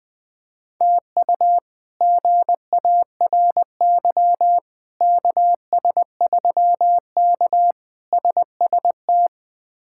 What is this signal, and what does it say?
Morse code, heard as a keyed tone, about 21 wpm, 700 Hz. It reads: TU GARY KS3K SST
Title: K1USN SST QSO Practice - 20wpm